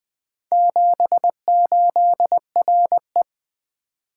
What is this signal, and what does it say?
Morse code, heard as a keyed tone, about 20 wpm, 700 Hz. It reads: 78RE
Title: Four Letter-Number Combinations - 20wpm